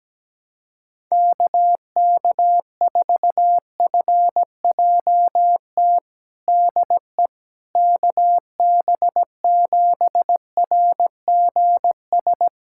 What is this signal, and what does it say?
Morse code, heard as a keyed tone, about 17 wpm, 700 Hz. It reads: KK4FJT DE KB7RGS
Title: USA Callsign DE Callsign - 6 Characters - 17wpm